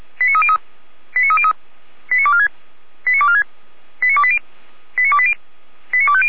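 Five-tone selective call (ZVEI), heard as quick runs of five tones, tones of about 70 ms. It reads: 89303 89303 89246 89246 89280 89280 89250
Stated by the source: ZVEI five-tone call received on SDR.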